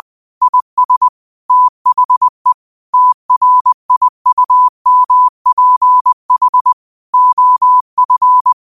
Morse code, heard as a keyed tone, about 20 wpm, 1000 Hz. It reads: IS THE TRIUMPH OF
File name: qotdfiles / 20wpm